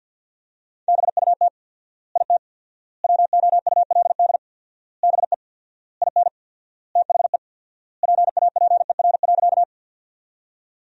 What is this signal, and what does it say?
Morse code, heard as a keyed tone, about 50 wpm, 700 Hz. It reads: BUT IT WOULD BE IN THE PAPER.
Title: 4x Word-Spacing - Sentences from Top 300 Words - 50wpm